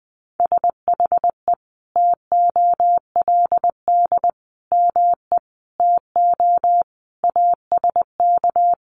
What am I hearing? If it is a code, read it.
Morse code, heard as a keyed tone, about 20 wpm, 700 Hz. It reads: SHE TOLD ME TO ASK
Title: Sentences from Top 400 Words 20wpm